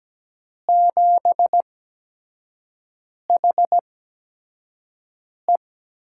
Morse code, heard as a keyed tone, about 17 wpm, spaced as for 5 wpm, 700 Hz. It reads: 7HE
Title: USA Call Signs - 4 Characters - 8x Character Spacing - 17wpm